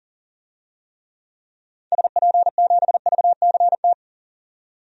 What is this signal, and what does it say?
Morse code, heard as a keyed tone, about 40 wpm, 700 Hz. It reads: SP7VCT
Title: DXCC-100 Call Signs - 6 Characters - 40wpm